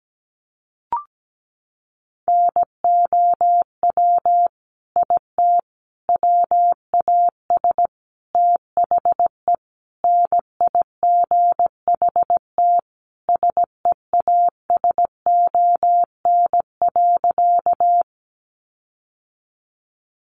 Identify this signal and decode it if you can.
Morse code, heard as a keyed tone, about 17 wpm, 700 Hz. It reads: NOW IT WAS THE NIGHT SEASON.
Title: Sentences from Top 1400 Words 17wpm